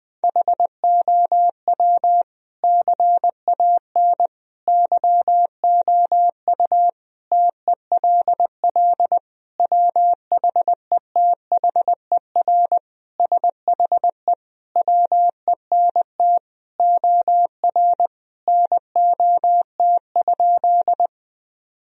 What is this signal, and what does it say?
Morse code, heard as a keyed tone, about 20 wpm, 700 Hz. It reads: HOW CAN YOU TELL WHETHER SHE WENT OR NOT?
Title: Sentences from Top 1000 Words - Review Part 3 - 20wpm